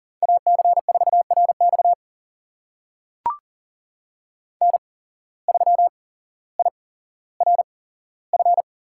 Morse code, heard as a keyed tone, about 40 wpm, spaced as for 17 wpm, 700 Hz. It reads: AC4RX N3IRF
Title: USA Call Signs - 8x Character Spacing - 40wpm